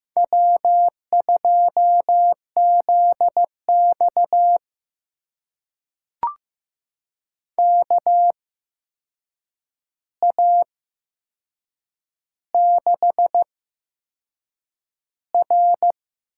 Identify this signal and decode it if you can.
Morse code, heard as a keyed tone, about 15 wpm, spaced as for 6 wpm, 700 Hz. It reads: W2ZX KA6R
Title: USA Call Signs - 4 Characters - 8x Character Spacing - 15wpm